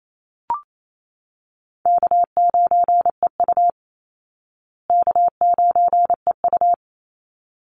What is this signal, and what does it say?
Morse code, heard as a keyed tone, about 28 wpm, 700 Hz. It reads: K9EU K9EU
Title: Easy Call Signs Repeated 2x - 28wpm